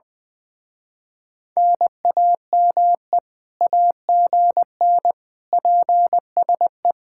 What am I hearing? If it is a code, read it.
Morse code, heard as a keyed tone, about 20 wpm, 700 Hz. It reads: NAME AGN PSE
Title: Speed-Racing Common QSO Phrases - 20wpm